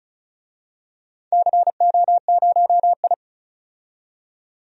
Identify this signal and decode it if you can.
Morse code, heard as a keyed tone, about 36 wpm, 700 Hz. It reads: CO0I
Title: DXCC-100 Call Signs - 4 Characters - 35wpm